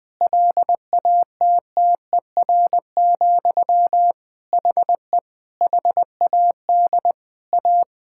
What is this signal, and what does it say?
Morse code, heard as a keyed tone, about 20 wpm, 700 Hz. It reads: LATTER, HE HAD A
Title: Sentences from Top 1600 Words 20wpm